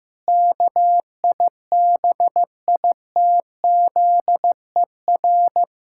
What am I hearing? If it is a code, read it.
Morse code, heard as a keyed tone, about 15 wpm, 700 Hz. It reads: KIBITZER
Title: Pangram Challenge v2 - 15wpm